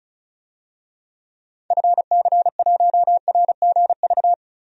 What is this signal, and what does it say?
Morse code, heard as a keyed tone, about 34 wpm, 700 Hz. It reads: FC1RGV
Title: DXCC-100 Call Signs - 6 Characters - 35wpm